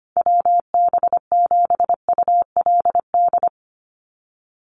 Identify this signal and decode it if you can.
Morse code, heard as a keyed tone, about 25 wpm, 700 Hz. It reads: WB7ULD
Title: NA SOTA Chaser Callsigns 25wpm